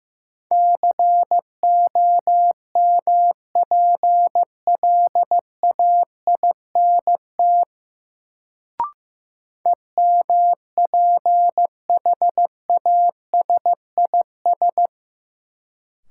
Morse code, heard as a keyed tone, about 15 wpm, 700 Hz. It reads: COMPLAINT EMPHASIS
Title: Sentences from Top 1400 Words - New Word Drill - 15wpm